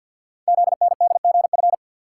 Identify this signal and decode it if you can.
Morse code, heard as a keyed tone, about 53 wpm, 700 Hz. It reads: ZNDGF
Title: Code Groups - 5 Letters - 50wpm